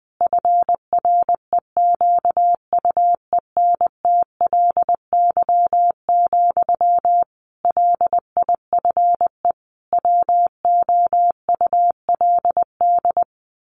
Morse code, heard as a keyed tone, about 20 wpm, 700 Hz. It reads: FREQUENTLY, LIFE WOULD
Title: Sentences from Top 1300 Words 20wpm